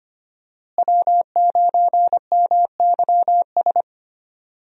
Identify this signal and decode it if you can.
Morse code, heard as a keyed tone, about 25 wpm, 690 Hz. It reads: W9MYS
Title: USA Call Signs - 5 Characters - Repeated 2x - 25wpm